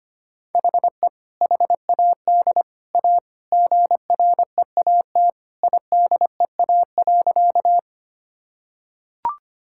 Morse code, heard as a keyed tone, about 25 wpm, 700 Hz. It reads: HE HAD A GREAT IDEA.
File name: practice / 25wpm / Sentences with 3-Word Phrases - 25wpm